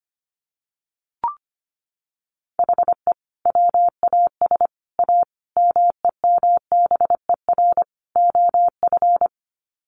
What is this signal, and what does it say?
Morse code, heard as a keyed tone, about 25 wpm, 700 Hz. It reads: HE WAS A MEMBER OF
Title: Sets of 5 Words Encore2 25wpm